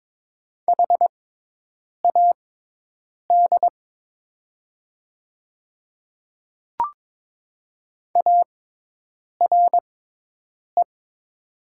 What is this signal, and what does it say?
Morse code, heard as a keyed tone, about 22 wpm, spaced as for 8 wpm, 700 Hz. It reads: HAD ARE
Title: 6x Character Spacing - Top 100 Words - 22wpm